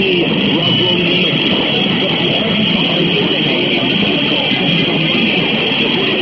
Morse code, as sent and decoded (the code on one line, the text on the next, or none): none